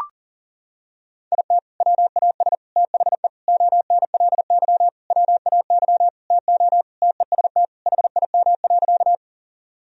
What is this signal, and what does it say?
Morse code, heard as a keyed tone, about 37 wpm, 700 Hz. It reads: IT WAS THE ONLY WAY TO TEST HIM.
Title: Sentences from Top 1000 Words - Review Part 3 - 40wpm